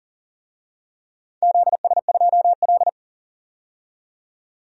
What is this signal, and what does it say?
Morse code, heard as a keyed tone, about 40 wpm, 700 Hz. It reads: ZS2L
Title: DXCC-100 Call Signs - 4 Characters - 40wpm